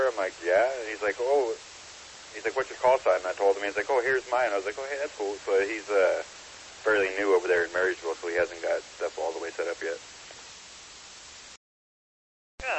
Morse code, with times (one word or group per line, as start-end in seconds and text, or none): none